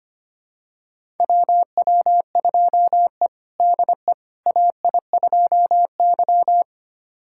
Morse code, heard as a keyed tone, about 25 wpm, 700 Hz. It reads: WW2E DE AI2Y
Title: USA Callsign DE Callsign - 4 Characters - 25wpm